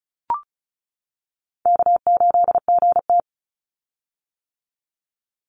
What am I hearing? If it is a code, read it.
Morse code, heard as a keyed tone, about 34 wpm, 700 Hz. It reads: K8GT
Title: Call Signs - 35wpm